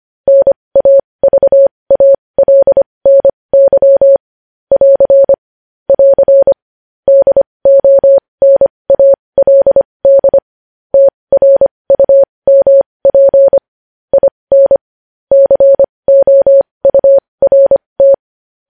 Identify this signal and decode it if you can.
Morse code, heard as a keyed tone, about 25 wpm, spaced as for 20 wpm, 550 Hz. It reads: NAVALNY <AR> <AR> DONALD TRUMP IN COURT
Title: News Headlines at 20